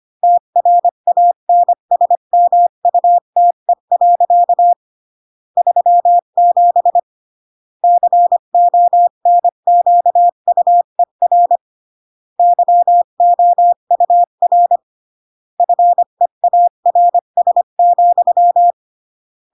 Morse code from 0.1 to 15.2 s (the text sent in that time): TRANSMUTE. 37 CONQUER YOUR